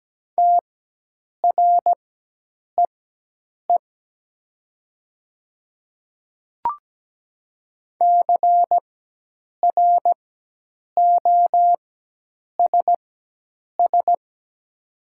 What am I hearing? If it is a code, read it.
Morse code, heard as a keyed tone, about 17 wpm, spaced as for 8 wpm, 700 Hz. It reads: TREE CROSS
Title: 4x Character Spacing - Top 300 Words - 17wpm